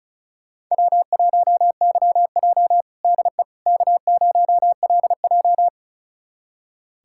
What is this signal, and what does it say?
Morse code, heard as a keyed tone, about 34 wpm, 700 Hz. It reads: W1YJ DE K0LJ
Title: USA Callsign DE Callsign - 4 Characters - 35wpm